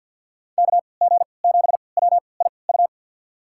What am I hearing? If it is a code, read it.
Morse code, heard as a keyed tone, about 52 wpm, spaced as for 29 wpm, 700 Hz. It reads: KG7WIU
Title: USA Call Signs - 6 Characters - 3x Character Spacing - 50wpm